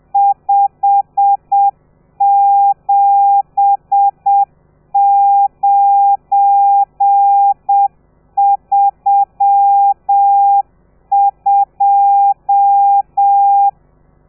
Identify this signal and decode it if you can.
Morse code, heard as a keyed tone, about 7 wpm, 800 Hz. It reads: 57932
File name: telegrafia / i2tfj / Lez12 30